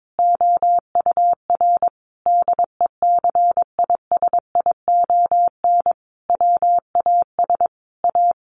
Morse code, heard as a keyed tone, about 22 wpm, 700 Hz. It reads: OUR DECISION WAS A
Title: Sentences from Top 1400 Words 22wpm